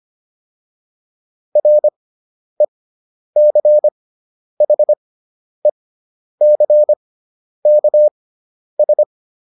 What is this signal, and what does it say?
Morse code, heard as a keyed tone, about 25 wpm, spaced as for 10 wpm, 600 Hz. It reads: RECHECKS